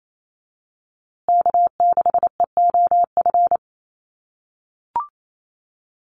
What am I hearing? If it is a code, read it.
Morse code, heard as a keyed tone, about 28 wpm, 700 Hz. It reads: K6EOF
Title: USA Call Signs Repeated 3x 28wpm